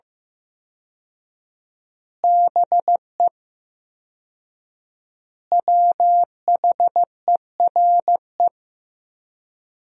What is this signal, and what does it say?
Morse code, heard as a keyed tone, about 15 wpm, 700 Hz. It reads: BE WHERE
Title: 4x Word-Spacing - Sentences from Top 200 Words - 15wpm